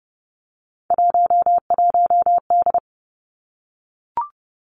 Morse code, heard as a keyed tone, about 30 wpm, 700 Hz. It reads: 11D
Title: Three Challenging Letter-Number Combinations 30wpm